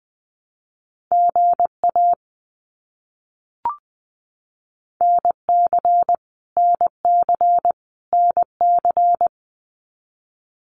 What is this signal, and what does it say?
Morse code, heard as a keyed tone, about 20 wpm, 700 Hz. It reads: GA NC NC NC
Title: US States Repeated 3x 20wpm